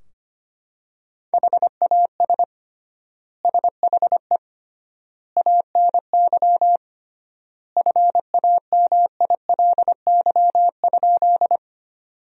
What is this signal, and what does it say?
Morse code, heard as a keyed tone, about 25 wpm, 700 Hz. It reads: HAS SHE ANY FAMILY?
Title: 3x Word-Spacing - Sentences from Top 400 Words - 25wpm